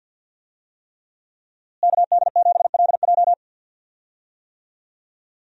Morse code, heard as a keyed tone, about 53 wpm, 700 Hz. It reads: KD7LJ
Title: USA Call Signs - 5 Characters - Repeated 3x - 50wpm